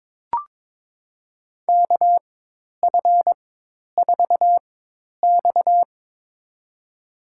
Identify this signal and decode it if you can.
Morse code, heard as a keyed tone, about 21 wpm, spaced as for 10 wpm, 700 Hz. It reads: KF4X
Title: USA Call Signs - 4 Characters - 4x Character Spacing - 22wpm